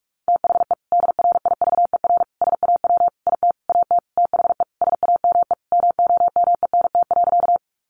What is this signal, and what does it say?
Morse code, heard as a keyed tone, about 44 wpm, 700 Hz. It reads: THE DRIVER SAW IT AT THE SAME MOMENT.
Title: Sentences from Top 1000 Words 45wpm